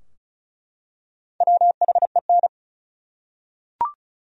Morse code, heard as a keyed tone, about 36 wpm, 700 Hz. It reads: WHEN
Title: Top 100 Words 35wpm